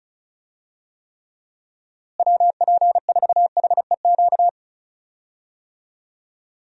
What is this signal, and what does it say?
Morse code, heard as a keyed tone, about 36 wpm, 700 Hz. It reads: WP4HEQ
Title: USA Call Signs - 6 Characters - Repeated 2x - 35wpm